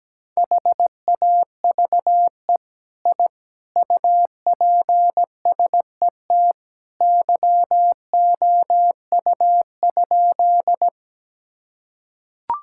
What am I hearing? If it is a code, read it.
Morse code, heard as a keyed tone, about 17 wpm, 700 Hz. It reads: HAVE I UPSET YOU?
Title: Sentences from Top 1000 Words - Review Part 5 - 17wpm